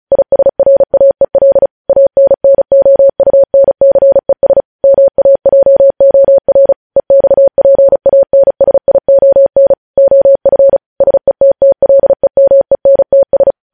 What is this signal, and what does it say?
Morse code, heard as a keyed tone, about 36 wpm, 550 Hz. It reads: ISRAEL ANNOUNCES MAJOR EXPANSION OF SETTLEMENTS
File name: podcast / News headlines-1x-35x35